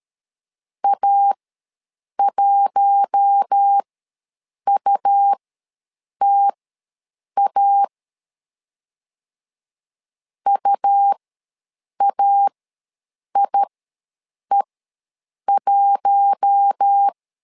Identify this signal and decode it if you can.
Morse code, heard as a keyed tone, about 13 wpm, spaced as for 7 wpm, 800 Hz. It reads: A1UTA UAIE1
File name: LEZIONI / 4 Q1 ws 12wpm 2129